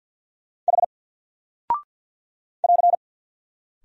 Morse code, heard as a keyed tone, about 48 wpm, 700 Hz. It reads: H <AR>
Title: Single Character (Letter:Number:Punctuation:Prosign) 50wpm